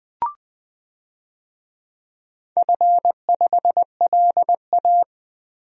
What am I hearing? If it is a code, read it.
Morse code, heard as a keyed tone, about 21 wpm, 700 Hz. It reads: F5LA
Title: DXCC-100 Call Signs - 4 Characters - Repeated 3x - 20wpm